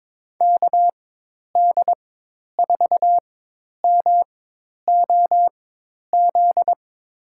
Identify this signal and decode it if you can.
Morse code, heard as a keyed tone, about 22 wpm, spaced as for 10 wpm, 700 Hz. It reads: KD4MOZ
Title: USA Call Signs - 6 Characters - 4x Character Spacing - 22wpm